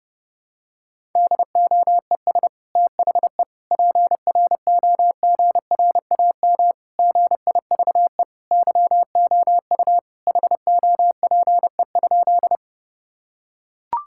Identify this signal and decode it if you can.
Morse code, heard as a keyed tone, about 30 wpm, 700 Hz. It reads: DOES THE PROGRAM GIVE YOU HOPE?
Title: Sentences from Top 1000 Words - Review Part 5 - 30wpm